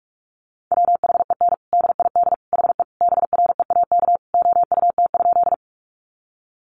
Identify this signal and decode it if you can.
Morse code, heard as a keyed tone, about 44 wpm, 700 Hz. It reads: WHEN DID HE BREAK OUT?
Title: Sentences from Top 900 Words 45wpm